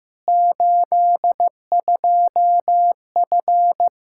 Morse code, heard as a keyed tone, about 15 wpm, 700 Hz. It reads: 82F
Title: Practice QSO - Exchange 2 - Info - 15wpm